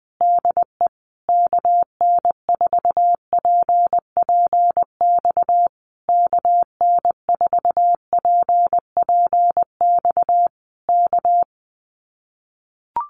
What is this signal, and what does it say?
Morse code, heard as a keyed tone, about 20 wpm, 700 Hz. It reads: DE KN4PPX KN4PPX K
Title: Common QSO Phrases Encore 20wpm